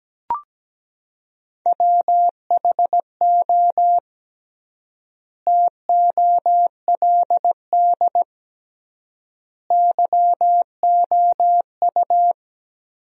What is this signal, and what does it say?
Morse code, heard as a keyed tone, about 17 wpm, 700 Hz. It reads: WHO TOLD YOU
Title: 3x Word-Spacing - Sentences from Top 400 Words - 17wpm